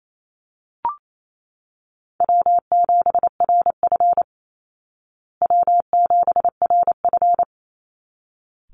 Morse code, heard as a keyed tone, about 28 wpm, 700 Hz. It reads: W7RF W7RF
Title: Easy Call Signs Repeated 2x - 28wpm